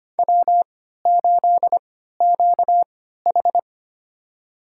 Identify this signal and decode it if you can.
Morse code, heard as a keyed tone, about 25 wpm, spaced as for 14 wpm, 700 Hz. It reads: W8QH
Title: USA Call Signs - 4 Characters - 3x Character Spacing - 25wpm